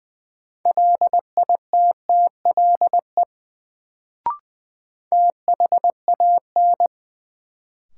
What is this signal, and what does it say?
Morse code, heard as a keyed tone, about 20 wpm, 700 Hz. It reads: LITTLE THAN
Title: Sets of 2 Words - New Word Drill - 20wpm